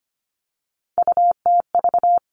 Morse code, heard as a keyed tone, about 25 wpm, 700 Hz. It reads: UTV
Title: All Cut Numbers - 3-Digit 25wpm